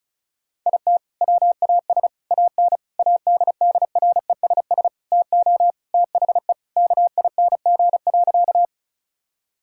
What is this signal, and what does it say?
Morse code, heard as a keyed tone, about 36 wpm, 700 Hz. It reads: IT WAS AN ADDRESS TO THE KING.
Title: Sentences from Top 1000 Words - Review Part 6 - 35wpm